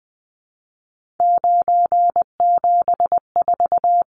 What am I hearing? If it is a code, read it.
Morse code, heard as a keyed tone, about 20 wpm, 700 Hz. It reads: 974
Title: Numbers - 3-Digit 20wpm